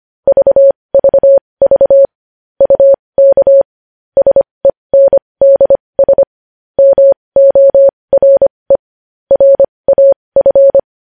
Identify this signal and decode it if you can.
Morse code, heard as a keyed tone, about 25 wpm, spaced as for 20 wpm, 550 Hz. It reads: VVV UK SENDS MORE RAF
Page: News headlines in Morse code at 20 WPM